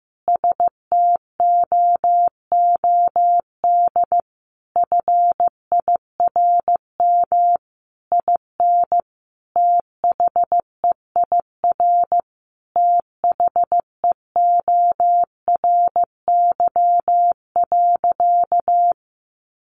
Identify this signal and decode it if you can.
Morse code, heard as a keyed tone, about 15 wpm, 700 Hz. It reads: STOOD FIRM IN THEIR THEORY.
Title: Sentences from Top 1400 Words 15wpm